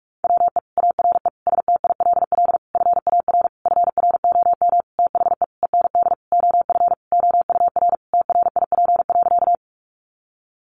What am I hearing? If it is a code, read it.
Morse code, heard as a keyed tone, about 44 wpm, 700 Hz. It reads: WE ARE STILL FAR FROM THE END OF OUR TRIP.
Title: Sentences from Top 1300 Words 45wpm